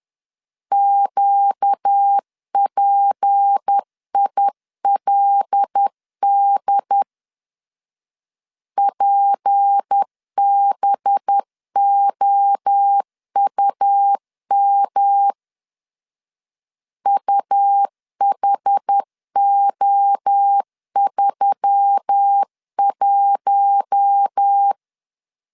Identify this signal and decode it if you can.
Morse code, heard as a keyed tone, about 10 wpm, 800 Hz. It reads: QPILD PBOUM UHO31
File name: LEZIONI / 6 PKX0 / 6 P 10wpm 2432